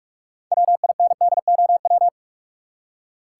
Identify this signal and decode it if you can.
Morse code, heard as a keyed tone, about 44 wpm, 700 Hz. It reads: WINDOW
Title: Six Letter Words - 45wpm